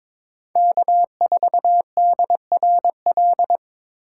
Morse code, heard as a keyed tone, about 25 wpm, 700 Hz. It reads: K4DRL
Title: Speed Racing - USA 1x3 Call Signs 22wpm